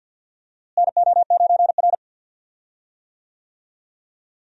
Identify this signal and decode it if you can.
Morse code, heard as a keyed tone, about 52 wpm, 700 Hz. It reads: NO9R
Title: USA Call Signs - 4 Characters - Repeated 2x - 50wpm